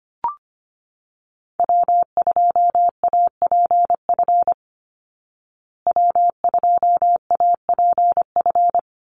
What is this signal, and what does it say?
Morse code, heard as a keyed tone, about 25 wpm, 700 Hz. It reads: W2APF W2APF
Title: Call Signs Repeated 2x - 25wpm